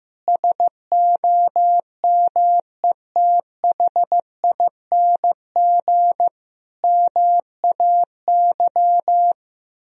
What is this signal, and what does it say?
Morse code, heard as a keyed tone, about 15 wpm, 700 Hz. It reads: SOMETHING MAY
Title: Sentences from Top 1000 Words - Review Part 3 - 15wpm